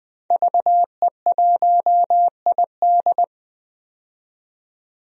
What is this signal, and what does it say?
Morse code, heard as a keyed tone, about 20 wpm, 700 Hz. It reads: VE1ID
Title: DXCC-100 Call Signs - 5 Characters - 20wpm